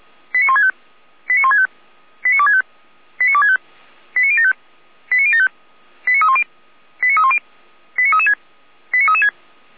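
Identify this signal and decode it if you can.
Five-tone selective call (ZVEI), heard as quick runs of five tones, tones of about 80 ms. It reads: 89165 89165 89265 89265 89075 89075 89310 89310 89336 89336
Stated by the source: ZVEI five-tone selective call received by SDR.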